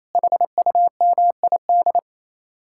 Morse code, heard as a keyed tone, about 34 wpm, 700 Hz. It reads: HUMID
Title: Speed-Racing Common QSO Phrases - 28wpm